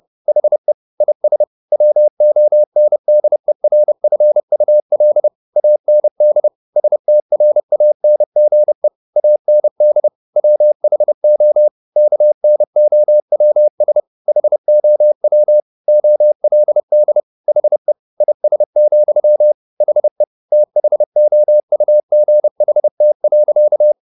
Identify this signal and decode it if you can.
Morse code, heard as a keyed tone, about 30 wpm, 600 Hz. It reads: HE IS WONDERFUL AND STRANGE AND WHO KNOWS HOW OLD HE IS, HE THOUGHT.
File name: the old man and the sea / the old man and the sea 1 30wpm